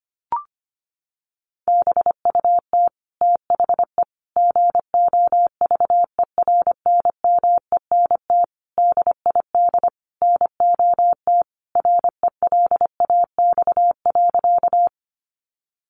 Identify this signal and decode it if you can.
Morse code, heard as a keyed tone, about 25 wpm, 700 Hz. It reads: BUT THE GOVERNMENT DID NOT RELAX.
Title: Sentences from Top 1400 Words 25wpm